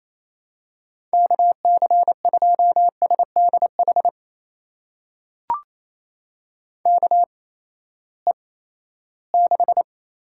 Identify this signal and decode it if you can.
Morse code, heard as a keyed tone, about 28 wpm, spaced as for 16 wpm, 700 Hz. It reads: KC2SDH KE6
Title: USA Call Signs - 6 Characters - 8x Character Spacing - 28wpm